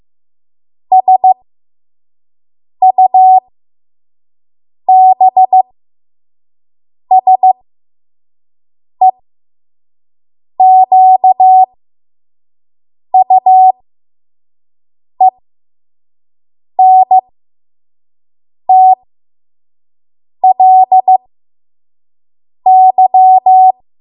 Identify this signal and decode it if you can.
Morse code, heard as a keyed tone, about 15 wpm, spaced as for 5 wpm, 750 Hz. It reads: SUBSEQUENTLY
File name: Morse / 05WPM